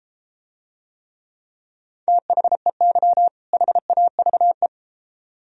Morse code, heard as a keyed tone, about 33 wpm, 700 Hz. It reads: THEY HAVE